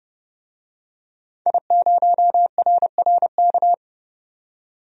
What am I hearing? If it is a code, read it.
Morse code, heard as a keyed tone, about 29 wpm, 690 Hz. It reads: I0RRK
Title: DXCC-100 Call Signs - 5 Characters - 30wpm